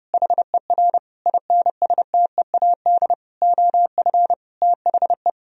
Morse code, heard as a keyed tone, about 30 wpm, 700 Hz. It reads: HER INSTEAD OF THE
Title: Sentences from Top 1000 Words - Review Part 4 - 30wpm